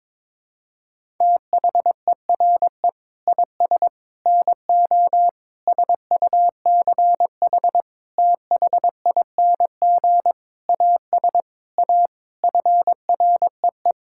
Morse code, heard as a keyed tone, about 22 wpm, 700 Hz. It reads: THERE IS NO SUCH THING AS A FREE
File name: Sentences with 9-Word Phrases - 22wpm